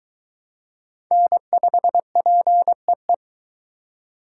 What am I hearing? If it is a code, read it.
Morse code, heard as a keyed tone, about 20 wpm, 700 Hz. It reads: N5PEE